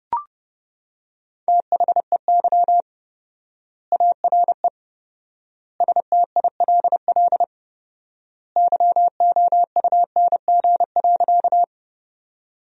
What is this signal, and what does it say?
Morse code, heard as a keyed tone, about 32 wpm, 700 Hz. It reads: THEY ARE STILL YOUNG.
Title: 4x Word-Spacing - Sentences from Top 400 Words - 30wpm